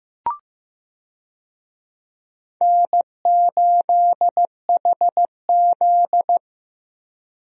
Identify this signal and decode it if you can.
Morse code, heard as a keyed tone, about 15 wpm, 700 Hz. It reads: N8HZ
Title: USA Call Signs - 4 Characters - Repeated 3x - 15wpm